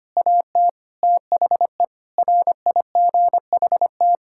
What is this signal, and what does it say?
Morse code, heard as a keyed tone, about 25 wpm, 700 Hz. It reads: AT THE RIGHT
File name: practice / 25wpm / Sentences with 9-Word Phrases - 25wpm